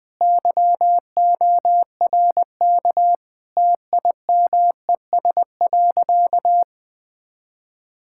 Morse code, heard as a keyed tone, about 21 wpm, 700 Hz. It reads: YORK TIMES.
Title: Sentences with 4-Word Phrases - 20wpm